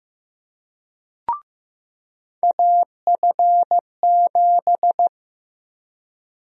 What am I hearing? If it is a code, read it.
Morse code, heard as a keyed tone, about 16 wpm, 700 Hz. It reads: AF7
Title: USA Call Sign Prefixes - Repeated 2x - 15wpm